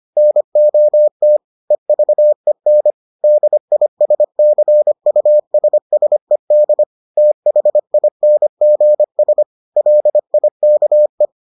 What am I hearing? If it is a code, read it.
Morse code, heard as a keyed tone, about 25 wpm, 600 Hz. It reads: NOT EVEN DISCUSSED THINGS LIKE